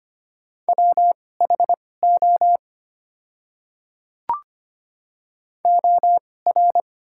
Morse code, heard as a keyed tone, about 25 wpm, spaced as for 18 wpm, 690 Hz. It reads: WHO OR
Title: 2x Character Spacing - Top 100 Words - 25wpm